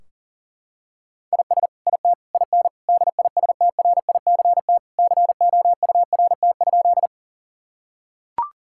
Morse code, heard as a keyed tone, about 37 wpm, 700 Hz. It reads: IS IT IN DISTRICT COURT?
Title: Sentences from Top 1600 Words 40wpm